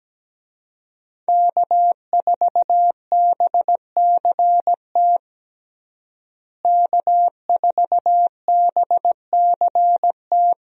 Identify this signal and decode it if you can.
Morse code, heard as a keyed tone, about 17 wpm, 700 Hz. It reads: K4BCT K4BCT
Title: USA Call Signs - 5 Characters - Repeated 3x - 17wpm